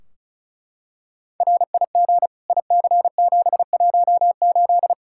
Morse code, heard as a keyed tone, about 34 wpm, 700 Hz. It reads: RIG IC718
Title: Practice QSO - Exchange 2 - Info - 35wpm